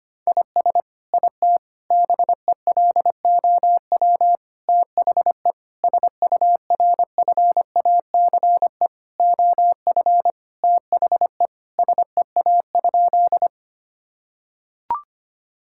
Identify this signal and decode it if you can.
Morse code, heard as a keyed tone, about 25 wpm, 700 Hz. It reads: IS IT BELOW THE SURFACE OF THE SEA?
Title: Sentences from Top 1000 Words - Review Part 5 - 25wpm